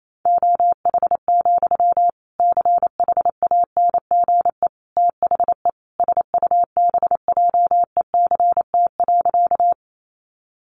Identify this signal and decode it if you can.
Morse code, heard as a keyed tone, about 28 wpm, 700 Hz. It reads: OH, CHANGE THE SUBJECT.
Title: Sentences from Top 800 Words 28wpm